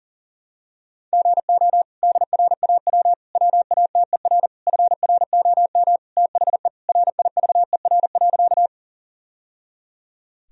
Morse code, heard as a keyed tone, about 40 wpm, 700 Hz. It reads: GO DRAW WATER FROM THE RIVER.
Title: Sentences from Top 1000 Words - Review Part 1 - 40wpm